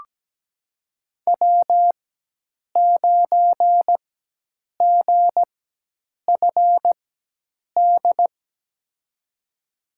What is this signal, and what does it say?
Morse code, heard as a keyed tone, about 17 wpm, spaced as for 8 wpm, 700 Hz. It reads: W9GFD
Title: USA Call Signs - 4x Character Spacing - 17wpm